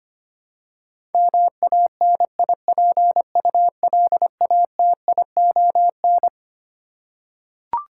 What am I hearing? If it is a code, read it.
Morse code, heard as a keyed tone, about 25 wpm, 700 Hz. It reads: MANIPULATION
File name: Twelve Letter Words - 25wpm